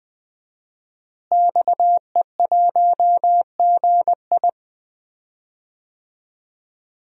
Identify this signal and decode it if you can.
Morse code, heard as a keyed tone, about 20 wpm, 700 Hz. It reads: XE1GI
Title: DXCC-100 Call Signs - 5 Characters - Repeated 2x - 20wpm